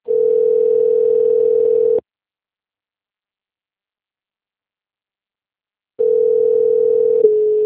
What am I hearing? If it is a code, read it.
Ringback tone (North America), 440 + 480 Hz.